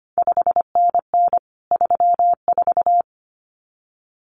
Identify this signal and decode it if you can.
Morse code, heard as a keyed tone, about 25 wpm, 700 Hz. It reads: 5NN 34
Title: CQ WW DX Contest Exchange Encore - 25wpm